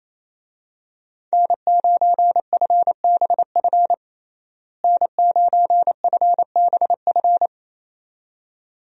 Morse code, heard as a keyed tone, about 27 wpm, 700 Hz. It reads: N9FBF N9FBF
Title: USA Call Signs - 5 Characters - Repeated 2x - 28wpm